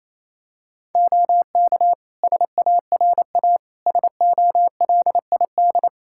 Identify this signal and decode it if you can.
Morse code, heard as a keyed tone, about 28 wpm, 700 Hz. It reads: OK SARA SOLID
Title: Practice QSO - Exchange 3 - More Info - 28wpm